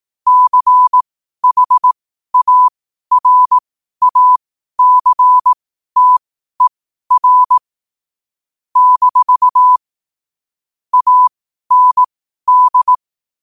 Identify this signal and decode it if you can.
Morse code, heard as a keyed tone, about 18 wpm, spaced as for 12 wpm, 1000 Hz. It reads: CHARACTER - AND
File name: qotdfiles / 12wpm